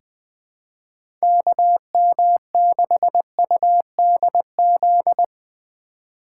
Morse code, heard as a keyed tone, about 20 wpm, 700 Hz. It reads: KM6UDZ
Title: USA Call Signs - 6 Characters - 3x Character Spacing - 20wpm